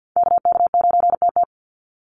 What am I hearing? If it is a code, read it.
Morse code, heard as a keyed tone, about 50 wpm, 700 Hz. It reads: KK9TT
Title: USA Call Signs 50wpm